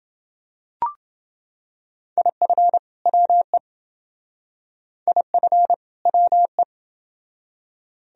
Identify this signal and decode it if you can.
Morse code, heard as a keyed tone, about 27 wpm, 700 Hz. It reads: IF WE IF WE